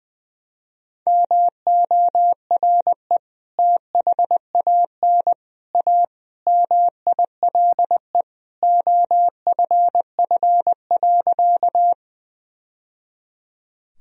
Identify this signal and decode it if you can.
Morse code, heard as a keyed tone, about 21 wpm, 700 Hz. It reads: MORE THAN A MILE OFF.
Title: Sentences from Top 1000 Words - Review Part 1 - 20wpm